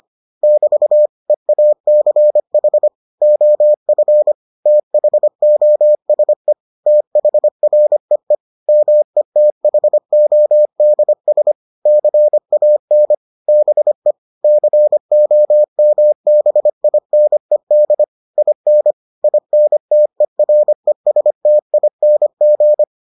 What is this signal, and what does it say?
Morse code, heard as a keyed tone, about 25 wpm, 600 Hz. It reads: <BT> EACH OF THOSE THREE METHODS CAN BE COMBINED IN INTERESTING